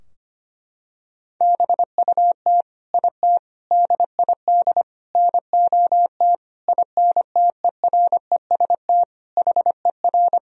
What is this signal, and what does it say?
Morse code, heard as a keyed tone, about 25 wpm, 700 Hz. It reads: BUT IT DID NOT INTEREST HER
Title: Sentences from Top 1000 Words - Review Part 2 - 25wpm